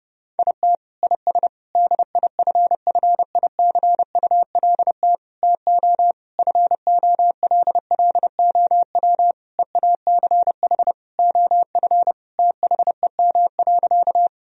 Morse code, heard as a keyed tone, about 30 wpm, 700 Hz. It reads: IT IS DIFFICULT TO FOLLOW EACH OF THEM.
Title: Sentences from Top 1000 Words - Review Part 1 - 30wpm